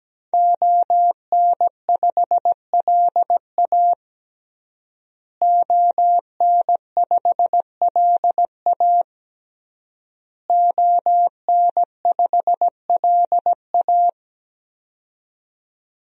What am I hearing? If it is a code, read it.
Morse code, heard as a keyed tone, about 17 wpm, 700 Hz. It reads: ON5LA ON5LA ON5LA
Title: DXCC-100 Call Signs - 5 Characters - Repeated 3x - 17wpm